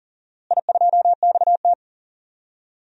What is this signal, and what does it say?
Morse code, heard as a keyed tone, about 40 wpm, 700 Hz. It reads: I2XT
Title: DXCC-100 Call Signs - 4 Characters - Repeated 2x - 40wpm